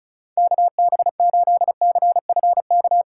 Morse code, heard as a keyed tone, about 45 wpm, 700 Hz. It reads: KB8CFK